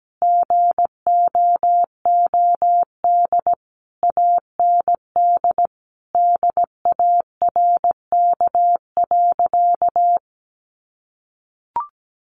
Morse code, heard as a keyed tone, about 17 wpm, 700 Hz. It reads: GOOD AND DARK.
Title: Sentences from Top 600 Words 17wpm